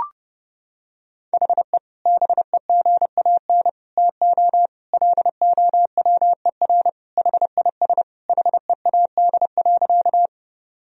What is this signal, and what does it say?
Morse code, heard as a keyed tone, about 30 wpm, 700 Hz. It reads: HE BEGAN TO LOWER HIS HEAD.
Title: Sentences from Top 1000 Words - Review Part 4 - 30wpm